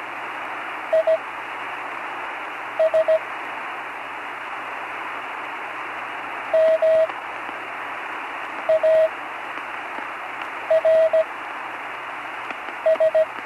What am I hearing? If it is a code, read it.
Morse code, heard as a keyed tone, about 17 wpm, spaced as for 5 wpm, 640 Hz, with radio noise. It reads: IS MARS
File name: CW Learning Programs / Learning Morse Code (from Tune in the World) / 43 on the air